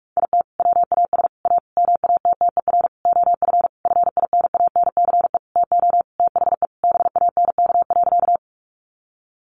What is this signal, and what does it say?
Morse code, heard as a keyed tone, about 44 wpm, 700 Hz. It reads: IT WAS A MATTER OF FINANCE TO THE BANK.
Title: Sentences from Top 1500 Words 45wpm